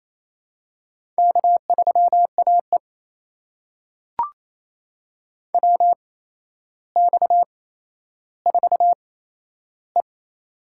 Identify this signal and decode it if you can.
Morse code, heard as a keyed tone, about 28 wpm, spaced as for 12 wpm, 700 Hz. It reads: K3AE WX4E
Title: USA Call Signs - 4 Characters - 8x Character Spacing - 28wpm